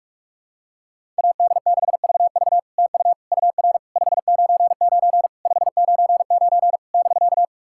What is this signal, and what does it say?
Morse code, heard as a keyed tone, about 48 wpm, 700 Hz. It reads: AD6VV TU UR 599 599 <BK>
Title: SOTA QSO Practice 45wpm